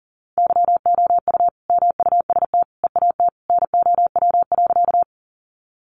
Morse code, heard as a keyed tone, about 40 wpm, 700 Hz. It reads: YOU MUST EAT NOW.
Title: Sentences from Top 300 Words 40wpm